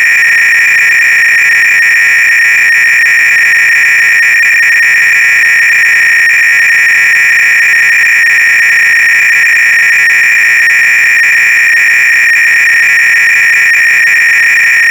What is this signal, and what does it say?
300-baud modem data.
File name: ACR-Audio-Cassette-Programs / CharlesSwiger-Donations / KING cload-k